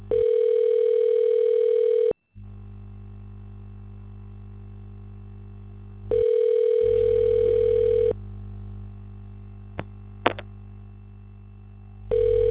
Ringback tone (North America), 440 + 480 Hz.
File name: recordings / RE7c2e82b a40ddaec41815d2eed